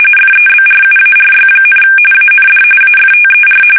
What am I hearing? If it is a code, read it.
RTTY, about 45.45 baud. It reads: FEBRUARY, AND AFTER
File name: fsk850 / USS-Massachusetts